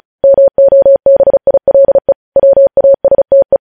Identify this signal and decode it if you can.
Morse code, heard as a keyed tone, about 36 wpm, 550 Hz. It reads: MOBILE WASTE
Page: news headlines in morse code at 35 wpm